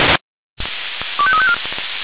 Five-tone selective call (ZVEI), heard as quick runs of five tones, tones of about 70 ms. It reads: 25354